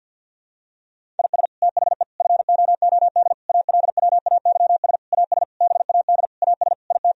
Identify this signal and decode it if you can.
Morse code, heard as a keyed tone, about 50 wpm, 700 Hz. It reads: IS THE FOOD ALWAYS AS BAD AS IT
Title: Sentences from Top 1000 Words - Review Part 1 - 50wpm